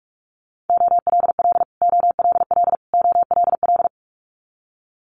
Morse code, heard as a keyed tone, about 46 wpm, 700 Hz. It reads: OLL OLL OLL
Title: Multi-letter Phonemes Repeated 3x 45wpm